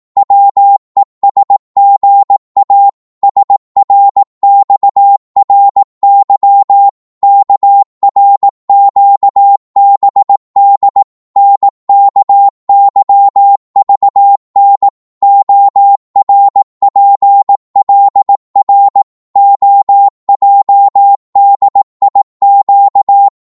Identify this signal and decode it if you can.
Morse code, heard as a keyed tone, about 18 wpm, 800 Hz. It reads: WESGA SRXRY KRQBD NKYVN ORPLR OJDIQ
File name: Morse / ltr-18wpm